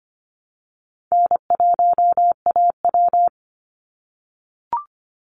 Morse code, heard as a keyed tone, about 25 wpm, 700 Hz. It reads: N1AW
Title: NA SOTA Chaser Callsigns 25wpm